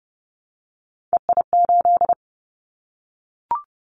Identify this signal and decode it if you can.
Morse code, heard as a keyed tone, about 31 wpm, 700 Hz. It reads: EI8
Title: Callsign Prefixes for Top 50 DXCC - 30wpm